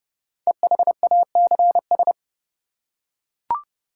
Morse code, heard as a keyed tone, about 30 wpm, 700 Hz. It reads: EHACS
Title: Code Groups - 5 Letters - 30wpm